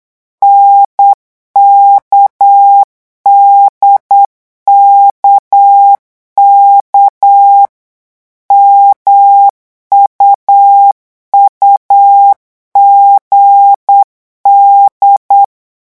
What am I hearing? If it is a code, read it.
Morse code, heard as a keyed tone, about 8 wpm, 800 Hz. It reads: NKDKK MUUGD
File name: morse mp3 / revision3-vitesse 8 mots